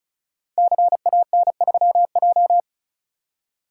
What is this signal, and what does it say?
Morse code, heard as a keyed tone, about 35 wpm, 700 Hz. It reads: CAN3J
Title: Code Groups - 5 Letter:Numbers - 35wpm